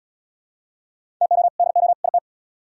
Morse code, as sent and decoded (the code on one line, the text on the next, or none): - --- / -.. --- / .. -
TO DO IT